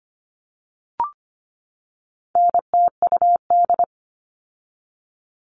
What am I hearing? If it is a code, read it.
Morse code, heard as a keyed tone, about 25 wpm, 700 Hz. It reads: NTUD
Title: All Cut Numbers - 4-Digit 25wpm